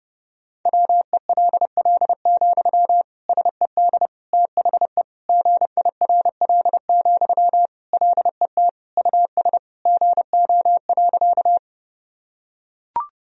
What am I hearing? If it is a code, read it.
Morse code, heard as a keyed tone, about 30 wpm, 700 Hz. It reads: WELL, SED THE GIRL, LET US GO.
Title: Sentences from Top 400 Words and Abbreviations 30wpm